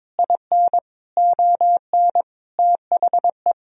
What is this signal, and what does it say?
Morse code, heard as a keyed tone, about 22 wpm, 700 Hz. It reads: IN ON THE
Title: Sentences from Top 1000 Words - Review Part 3 - 22wpm